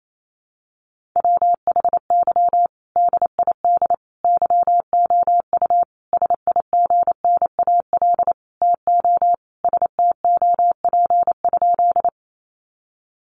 Morse code, heard as a keyed tone, about 28 wpm, 700 Hz. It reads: WHY DID YOU SIGNAL TO STOP?
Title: Sentences from Top 1400 Words 28wpm